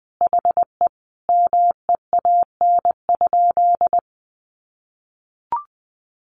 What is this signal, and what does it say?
Morse code, heard as a keyed tone, about 20 wpm, 700 Hz. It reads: HE MEAN?
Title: Sentences from Top 200 Words 20wpm